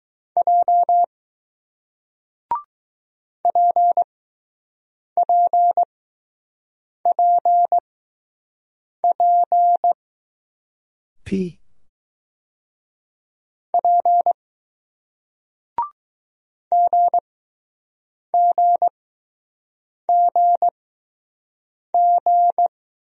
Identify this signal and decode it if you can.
Morse code, heard as a keyed tone, about 20 wpm, spaced as for 6 wpm, 700 Hz. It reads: J PPPP P GGGG